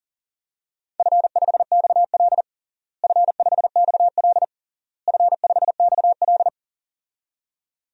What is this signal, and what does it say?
Morse code, heard as a keyed tone, about 37 wpm, 700 Hz. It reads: F5XL F5XL F5XL
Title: DXCC-100 Call Signs - 4 Characters - Repeated 3x - 40wpm